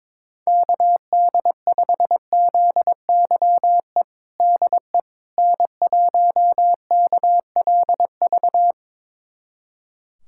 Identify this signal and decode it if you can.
Morse code, heard as a keyed tone, about 22 wpm, 700 Hz. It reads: KD5ZYE DE N1KLV
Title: USA Callsign DE Callsign 22wpm